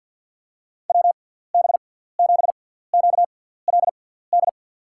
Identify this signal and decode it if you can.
Morse code, heard as a keyed tone, about 53 wpm, spaced as for 17 wpm, 700 Hz. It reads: WB7QLD
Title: USA Call Signs - 6 Characters - 6x Character Spacing - 50wpm